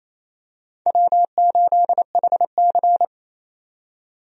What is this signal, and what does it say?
Morse code, heard as a keyed tone, about 28 wpm, 700 Hz. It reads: W8HC
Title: USA Call Signs - 4 Characters - 28wpm